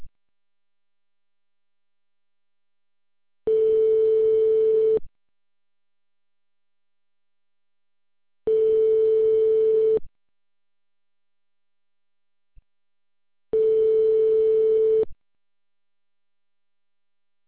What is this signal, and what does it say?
Ringback tone (France), 440 Hz.